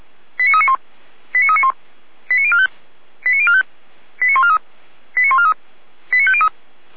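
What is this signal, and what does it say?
Five-tone selective call (ZVEI), heard as quick runs of five tones, tones of about 70 ms. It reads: 89391 89391 89045 89045 89143 89143 89503
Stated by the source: ZVEI five-tone selective call 89042 received by SDR.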